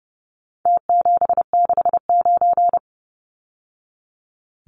Morse code, heard as a keyed tone, about 30 wpm, 700 Hz. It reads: T769
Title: Numbers - Contest Style - 4-Digit 30wpm